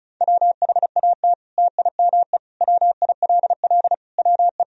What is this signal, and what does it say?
Morse code, heard as a keyed tone, about 34 wpm, 700 Hz. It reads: WHAT TIME WILL WE
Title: Sentences from Top 1000 Words - Review Part 1 - 35wpm